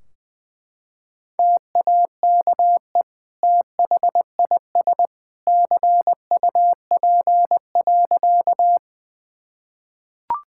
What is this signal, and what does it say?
Morse code, heard as a keyed tone, about 19 wpm, 700 Hz. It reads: TAKE THIS CUP.
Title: Sentences from Top 1600 Words 20wpm